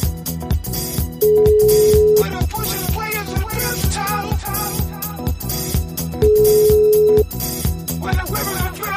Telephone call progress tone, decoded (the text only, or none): ringback tone